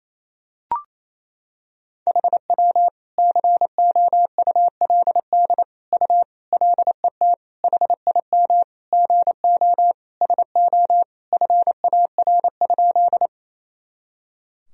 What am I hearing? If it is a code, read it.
Morse code, heard as a keyed tone, about 28 wpm, 700 Hz. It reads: HW COULD U LET HIM GO SO FAR?
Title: Sentences from Top 300 Words and Abbreviations 28wpm